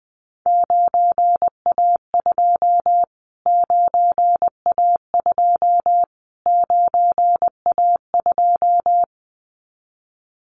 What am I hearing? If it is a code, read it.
Morse code, heard as a keyed tone, about 20 wpm, 700 Hz. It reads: 9A2 9A2 9A2
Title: Callsign Prefixes for Top 25 DXCC - Repeated 3x - 20wpm